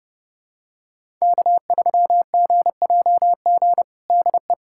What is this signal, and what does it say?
Morse code, heard as a keyed tone, about 30 wpm, 700 Hz. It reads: K3GJG DE
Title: USA Callsign DE Callsign - 5 Characters - 30wpm